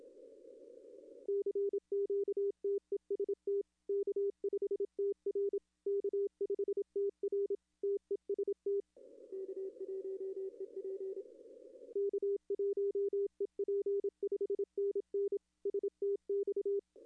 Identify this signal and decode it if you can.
Morse code, heard as a keyed tone, about 28 wpm, 410 Hz. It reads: CQ TEST K5TR K5TR TEST K1EP K1EP 5NN STX
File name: contests / arrl-160-k5tr-op-w5tn -0800